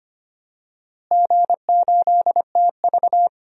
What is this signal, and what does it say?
Morse code, heard as a keyed tone, about 25 wpm, 700 Hz. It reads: G8TV
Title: DXCC-100 Call Signs - 4 Characters - Repeated 2x - 25wpm